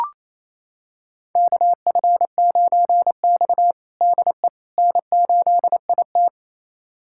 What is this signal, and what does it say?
Morse code, heard as a keyed tone, about 28 wpm, 700 Hz. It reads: KF9X DE N8IT
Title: USA Callsign DE Callsign - 4 Characters - 28wpm